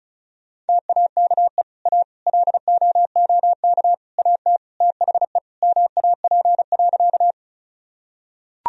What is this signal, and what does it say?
Morse code, heard as a keyed tone, about 34 wpm, 700 Hz. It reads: TAKE A LOOK AT THE MAP.
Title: Sentences from Top 1000 Words - Review Part 2 - 35wpm